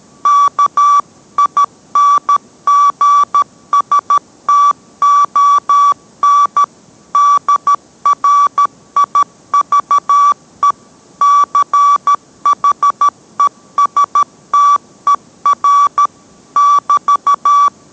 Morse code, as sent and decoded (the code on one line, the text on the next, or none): -.- .. -. --. ... - --- -. / -.. .-. .. ...- . / -.-. .... ... - . .-. / -...-
KINGSTON DRIVE CHESTER <BT>